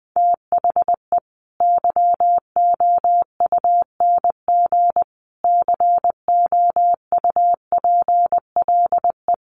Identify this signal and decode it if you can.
Morse code, heard as a keyed tone, about 20 wpm, 700 Hz. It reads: THE YOUNG COUPLE
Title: Sentences from Top 1100 Words 20wpm